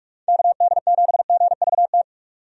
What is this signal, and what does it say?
Morse code, heard as a keyed tone, about 46 wpm, 700 Hz. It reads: KD7GVT
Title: USA Call Signs - 6 Characters - 45wpm